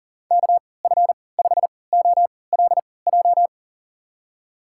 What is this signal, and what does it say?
Morse code, heard as a keyed tone, about 40 wpm, spaced as for 23 wpm, 700 Hz. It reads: KF5OLJ
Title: USA Call Signs - 6 Characters - 3x Character Spacing - 40wpm